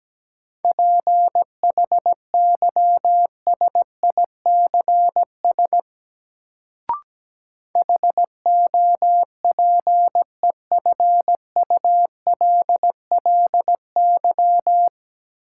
Morse code, heard as a keyed tone, about 17 wpm, 700 Hz. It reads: PHYSICS HOPEFULLY
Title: Sentences from Top 1500 Words - New Word Drill Alt - 17wpm